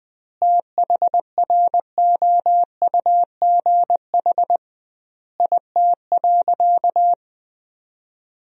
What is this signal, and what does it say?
Morse code, heard as a keyed tone, about 20 wpm, 700 Hz. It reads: THROUGH IT.
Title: 2x Word-Spacing - Sentences from Top 200 Words - 20wpm